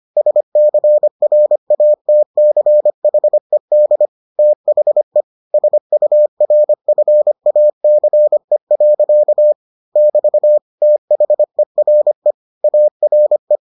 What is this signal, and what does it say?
Morse code, heard as a keyed tone, about 25 wpm, 600 Hz. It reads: SCRATCHED THE SURFACE. <BT> THERE ARE